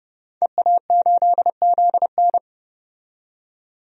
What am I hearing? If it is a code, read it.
Morse code, heard as a keyed tone, about 30 wpm, 700 Hz. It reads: EA8ZN
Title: DXCC-100 Call Signs - 5 Characters - 30wpm